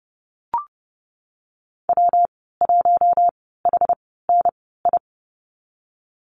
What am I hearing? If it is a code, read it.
Morse code, heard as a keyed tone, about 30 wpm, spaced as for 17 wpm, 700 Hz. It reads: W1HNI
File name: Lesson 19g - Letter w - Call Signs 30wpm